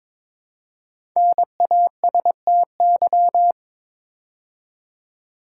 Morse code, heard as a keyed tone, about 22 wpm, 690 Hz. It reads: NASTY
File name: 5 Letter Words - First 3 Letters Given - 22wpm